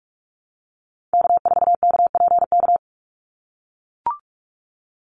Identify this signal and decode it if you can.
Morse code, heard as a keyed tone, about 44 wpm, 700 Hz. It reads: K4KPK
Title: NA SOTA Activator Callsigns 45wpm